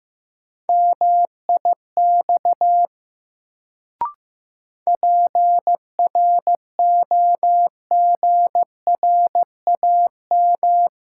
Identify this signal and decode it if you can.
Morse code, heard as a keyed tone, about 15 wpm, 700 Hz. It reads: MIX PROGRAM
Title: Sentences from Top 1000 Words - Review Part 4 - New Word Drill - 15wpm